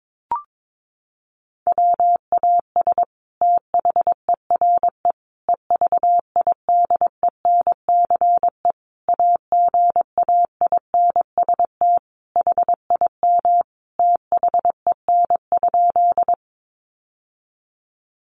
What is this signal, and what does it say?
Morse code, heard as a keyed tone, about 22 wpm, 700 Hz. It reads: WAS THERE EVIDENCE AGAINST HIM THEN?
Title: Sentences from Top 1300 Words 22wpm